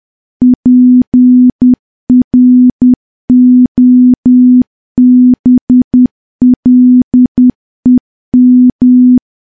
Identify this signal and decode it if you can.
Morse code, heard as a keyed tone, about 10 wpm, 250 Hz. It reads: PROBLEM
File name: morse / morse-training-10WPM